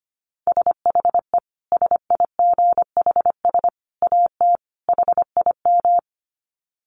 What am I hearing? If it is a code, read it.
Morse code, heard as a keyed tone, about 25 wpm, 700 Hz. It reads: SHE SIGHS AT HIM
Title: Dit Challenge 25wpm